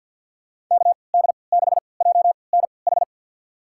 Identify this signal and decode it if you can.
Morse code, heard as a keyed tone, about 52 wpm, spaced as for 29 wpm, 700 Hz. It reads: KD6JNH
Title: USA Call Signs - 3x Character Spacing - 50wpm